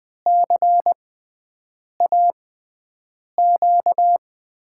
Morse code, heard as a keyed tone, about 20 wpm, spaced as for 7 wpm, 700 Hz. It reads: CAQ
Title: USA Call Signs - 6 Characters - 6x Character Spacing - 20wpm